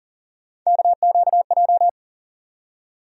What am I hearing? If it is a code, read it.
Morse code, heard as a keyed tone, about 40 wpm, 700 Hz. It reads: KQJ
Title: USA Call Sign Postfixes - 3 Letters 40wpm